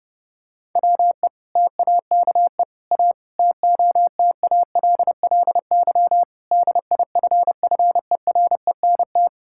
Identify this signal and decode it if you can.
Morse code, heard as a keyed tone, about 32 wpm, 700 Hz. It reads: WE TAKE A TOTALLY DIFFERENT
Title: Sentences from Top 1600 Words 30wpm